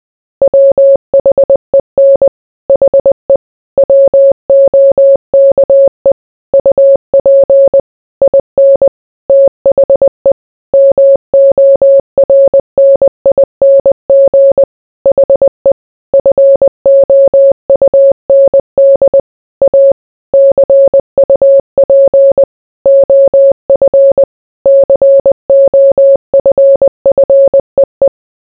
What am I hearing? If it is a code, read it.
Morse code, heard as a keyed tone, about 20 wpm, 550 Hz. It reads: WHEN HE WOKE UP IN THE MORNING HE FOUND A CUP OF COFFEE